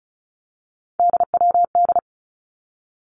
Morse code, heard as a keyed tone, about 34 wpm, 700 Hz. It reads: DWD
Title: Three Challenging Letter-Number Combinations 35wpm